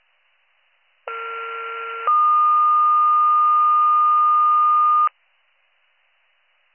Two-tone page, heard as a one-second tone, then a three-second tone, about 1370 then 1220 Hz. It is heard from 1.1 to 5.1 s.